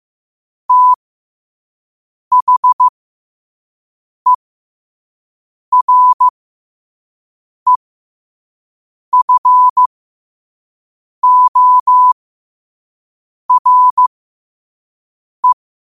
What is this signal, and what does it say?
Morse code, heard as a keyed tone, about 15 wpm, spaced as for 5 wpm, 1000 Hz. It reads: THEREFORE
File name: qotdfiles / 5wpm